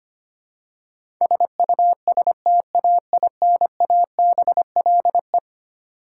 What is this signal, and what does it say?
Morse code, heard as a keyed tone, about 25 wpm, 700 Hz. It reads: SUSTAINABLE
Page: morse-code-ninja-practice-25wpm